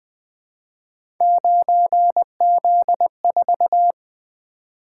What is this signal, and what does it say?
Morse code, heard as a keyed tone, about 20 wpm, 700 Hz. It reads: 9Z4
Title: Three Letter-Number Combinations - 20wpm